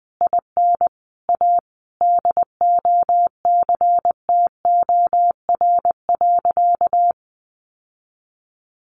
Morse code, heard as a keyed tone, about 20 wpm, 700 Hz. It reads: IN A DOCTOR.
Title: Sentences from Top 1500 Words 20wpm